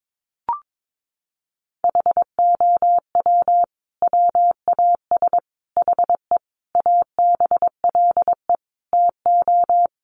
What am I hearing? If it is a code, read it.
Morse code, heard as a keyed tone, about 22 wpm, 700 Hz. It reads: HOW WAS HE ABLE TO
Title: Sentences from Top 1400 Words 22wpm